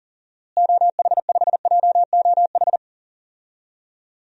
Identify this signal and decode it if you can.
Morse code, heard as a keyed tone, about 37 wpm, 700 Hz. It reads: OH5JOH
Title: DXCC-100 Call Signs - 6 Characters - Repeated 2x - 40wpm